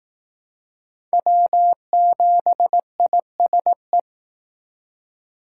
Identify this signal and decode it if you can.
Morse code, heard as a keyed tone, about 18 wpm, 700 Hz. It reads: W7ISE
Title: Speed Racing - USA 1x3 Call Signs 15wpm